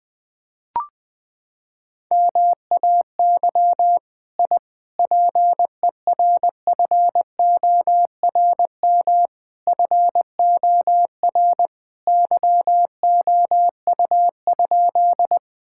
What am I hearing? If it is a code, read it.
Morse code, heard as a keyed tone, about 20 wpm, 700 Hz. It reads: MAY I PERFORM FOR YOU?
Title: Sentences from Top 1000 Words - Review Part 5 - 20wpm